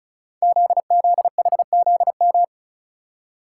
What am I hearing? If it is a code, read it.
Morse code, heard as a keyed tone, about 36 wpm, 700 Hz. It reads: ZZHZM
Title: Code Groups - 5 Letters - 35wpm